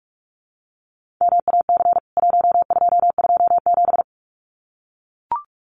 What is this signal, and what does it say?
Morse code, heard as a keyed tone, about 44 wpm, 700 Hz. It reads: MAC 1227
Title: CWT Exchange 45wpm